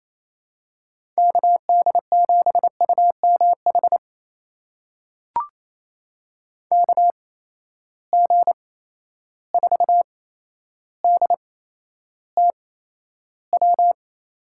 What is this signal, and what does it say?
Morse code, heard as a keyed tone, about 28 wpm, spaced as for 12 wpm, 700 Hz. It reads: KD7UMH KG4DTW
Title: USA Call Signs - 6 Characters - 8x Character Spacing - 28wpm